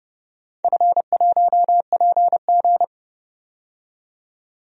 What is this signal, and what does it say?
Morse code, heard as a keyed tone, about 30 wpm, 700 Hz. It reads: F1PG
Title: DXCC-100 Call Signs - 4 Characters - 30wpm